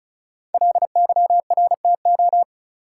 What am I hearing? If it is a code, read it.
Morse code, heard as a keyed tone, about 36 wpm, 700 Hz. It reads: LYRTO
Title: Code Groups - 5 Letters - 35wpm